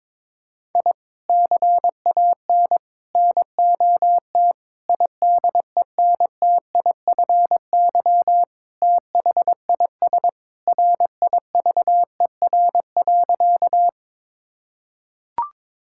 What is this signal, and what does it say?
Morse code, heard as a keyed tone, about 22 wpm, 700 Hz. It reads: I CAN NOT IDENTIFY THIS RIVER.
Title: Sentences from Top 1000 Words - Review Part 6 - 22wpm